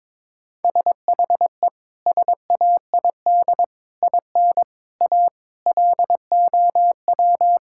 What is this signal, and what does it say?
Morse code, heard as a keyed tone, about 21 wpm, 700 Hz. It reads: SHE SAID IN A LOW
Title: Sentences from Top 1600 Words 22wpm